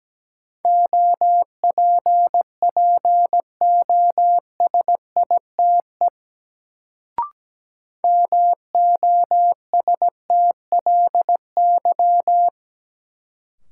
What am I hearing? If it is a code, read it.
Morse code, heard as a keyed tone, about 17 wpm, 700 Hz. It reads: OPPOSITE MOSTLY
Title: Sentences from Top 1300 Words - New Word Drill Alt - 17wpm